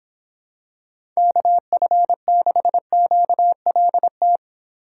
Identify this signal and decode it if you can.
Morse code, heard as a keyed tone, about 26 wpm, 700 Hz. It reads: KF6QLT